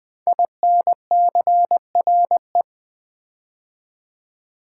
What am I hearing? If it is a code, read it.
Morse code, heard as a keyed tone, about 20 wpm, 700 Hz. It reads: INCRE
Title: Build A Word - 5 Letters At A Time - 12-Letter Words - 20wpm